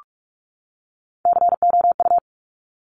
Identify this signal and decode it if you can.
Morse code, heard as a keyed tone, about 46 wpm, 700 Hz. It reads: COU
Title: Top Three Letter Combinations in Words 45wpm